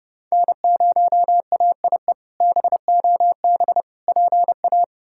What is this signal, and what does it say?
Morse code, heard as a keyed tone, about 31 wpm, 700 Hz. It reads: N0AIE BOB PA
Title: K1USN SST QSO Practice - 30wpm